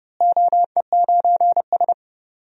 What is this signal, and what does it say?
Morse code, heard as a keyed tone, about 30 wpm, 700 Hz. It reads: OE9S
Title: DXCC-100 Call Signs - 4 Characters - Repeated 3x - 30wpm